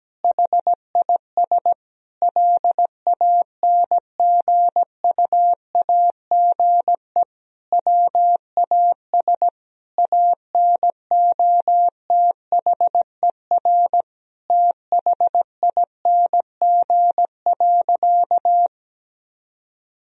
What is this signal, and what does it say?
Morse code, heard as a keyed tone, about 17 wpm, 700 Hz. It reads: HIS LANGUAGE WAS ANOTHER THING.
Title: Sentences from Top 1000 Words - Review Part 3 - 17wpm